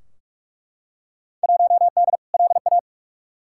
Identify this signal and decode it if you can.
Morse code, heard as a keyed tone, about 42 wpm, 700 Hz. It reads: 1D LA
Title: Field Day Exchanges - 45wpm